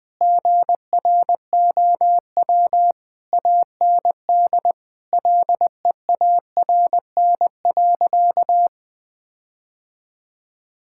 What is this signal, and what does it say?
Morse code, heard as a keyed tone, about 20 wpm, 700 Hz. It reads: GROW AND LEARN.
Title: Difficult Sentences with Hints - 20wpm